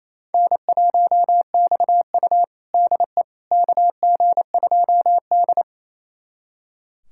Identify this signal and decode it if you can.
Morse code, heard as a keyed tone, about 28 wpm, 700 Hz. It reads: N1XU DE KG2D
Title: USA Callsign DE Callsign - 4 Characters - 28wpm